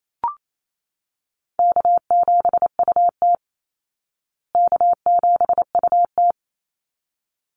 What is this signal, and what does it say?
Morse code, heard as a keyed tone, about 28 wpm, 700 Hz. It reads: K7UT K7UT
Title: Easy Call Signs Repeated 2x - 28wpm